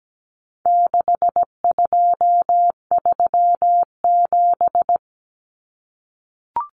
Morse code, heard as a keyed tone, about 17 wpm, 700 Hz. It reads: 6237
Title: Numbers - 4-Digit 17wpm